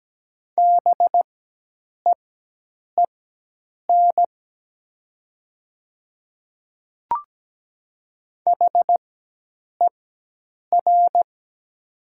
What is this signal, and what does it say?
Morse code, heard as a keyed tone, about 17 wpm, spaced as for 8 wpm, 700 Hz. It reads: BEEN HER
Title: 4x Character Spacing - Top 100 Words - 17wpm